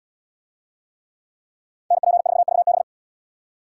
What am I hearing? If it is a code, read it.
Morse code, heard as a keyed tone, about 77 wpm, 700 Hz. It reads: NP4VB